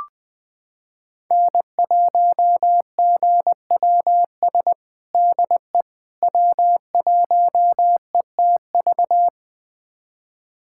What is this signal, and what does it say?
Morse code, heard as a keyed tone, about 20 wpm, 700 Hz. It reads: N1GWS DE W1ETV
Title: USA Callsign DE Callsign - 5 Characters - 20wpm